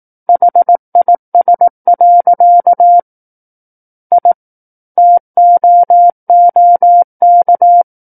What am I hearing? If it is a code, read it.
Morse code, heard as a keyed tone, about 18 wpm, 700 Hz. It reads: HIS. I TOOK
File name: 18 WPM